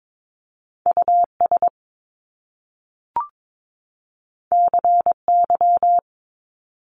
Morse code, heard as a keyed tone, about 22 wpm, 700 Hz. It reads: US CY
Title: Top Two Letter Combinations in Words 22wpm